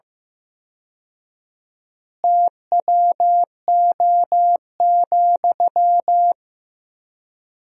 Morse code, heard as a keyed tone, about 16 wpm, 700 Hz. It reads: TWO,
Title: 4x Word-Spacing - Sentences from Top 200 Words - 15wpm